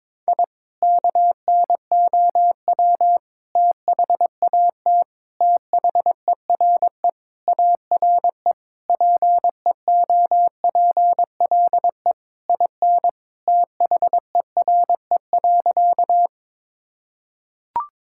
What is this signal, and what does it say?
Morse code, heard as a keyed tone, about 22 wpm, 700 Hz. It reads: I KNOW THAT THERE ARE PEOPLE IN THERE.
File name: practice / 22wpm / Sentences with 3-Word Phrases - 22wpm